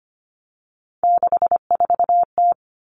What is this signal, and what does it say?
Morse code, heard as a keyed tone, about 25 wpm, 700 Hz. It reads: 64T
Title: Common Cut Numbers - 3-Digit 25wpm